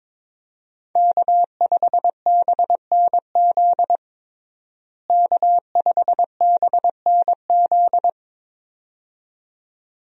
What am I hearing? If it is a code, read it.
Morse code, heard as a keyed tone, about 22 wpm, 700 Hz. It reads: K5BNZ K5BNZ
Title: USA Call Signs - 5 Characters - Repeated 2x - 22wpm